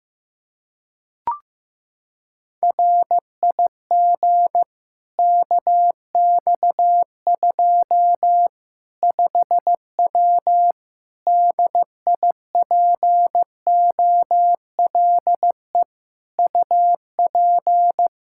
Morse code, heard as a keyed tone, about 15 wpm, 700 Hz. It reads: RIG KX2 5W DIPOLE UP
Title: Practice QSO - Exchange 2 - Info - 15wpm